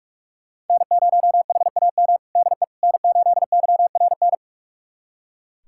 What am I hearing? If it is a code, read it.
Morse code, heard as a keyed tone, about 44 wpm, 700 Hz. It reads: N0HAM DE N8YRN
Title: USA Callsign DE Callsign - 5 Characters - 45wpm